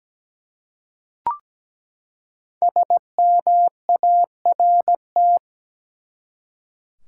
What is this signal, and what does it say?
Morse code, heard as a keyed tone, about 17 wpm, 700 Hz. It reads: SMART
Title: Five Letter Words - 17wpm